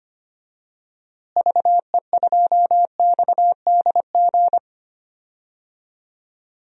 Morse code, heard as a keyed tone, about 25 wpm, 700 Hz. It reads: VE2XDG
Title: DXCC-100 Call Signs - 6 Characters - Repeated 3x - 25wpm